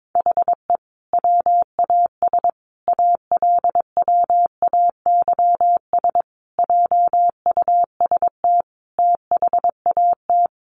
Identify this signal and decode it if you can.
Morse code, heard as a keyed tone, about 22 wpm, 700 Hz. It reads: HE WAS ALWAYS JUST THAT
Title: Sentences from Top 600 Words 22wpm